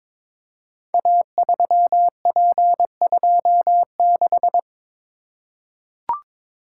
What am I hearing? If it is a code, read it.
Morse code, heard as a keyed tone, about 23 wpm, 700 Hz. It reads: A3P26
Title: Code Groups - 5 Letter:Numbers - 22wpm